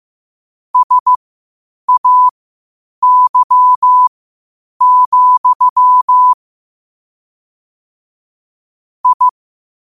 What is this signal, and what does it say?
Morse code, heard as a keyed tone, about 15 wpm, spaced as for 8 wpm, 1000 Hz. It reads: SAY, I
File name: qotdfiles / 8wpm